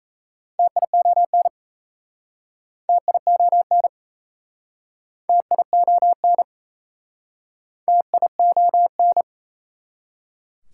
Morse code, heard as a keyed tone, about 37 wpm, 700 Hz. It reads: TION TION TION TION